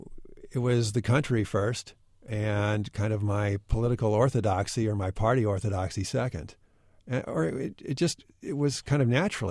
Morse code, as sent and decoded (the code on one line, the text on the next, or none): none